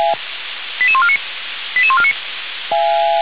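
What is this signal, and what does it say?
Five-tone selective call (ZVEI), heard as quick runs of five tones, tones of about 70 ms. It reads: 88149 88149
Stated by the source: ZVEI five-tone selective call received on an SDR.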